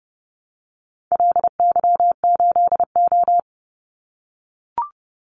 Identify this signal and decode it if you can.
Morse code, heard as a keyed tone, about 30 wpm, 700 Hz. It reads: LY8O
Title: Call Signs - 30wpm